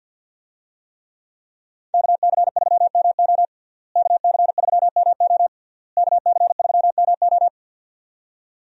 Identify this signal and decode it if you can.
Morse code, heard as a keyed tone, about 51 wpm, 700 Hz. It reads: KC3MO KC3MO KC3MO
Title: USA Call Signs - 5 Characters - Repeated 3x - 50wpm